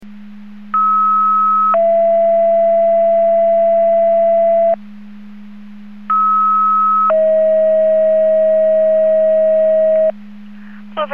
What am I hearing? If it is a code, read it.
Two-tone page, heard as a one-second tone, then a three-second tone, about 1290 then 670 Hz, about 1290 then 640 Hz. It is heard from 0.7 to 10.1 s.
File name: Portage 154.385 Paging